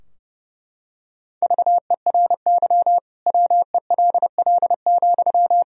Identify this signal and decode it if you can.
Morse code, heard as a keyed tone, about 30 wpm, 700 Hz. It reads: VERY WELL,
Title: Sentences from Top 1000 Words - Review Part 4 - 30wpm